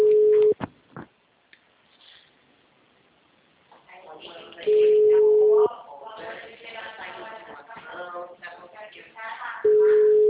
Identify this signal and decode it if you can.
Ringback tone (Europe, 425 Hz), with speech.
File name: out- -2036- - - .16317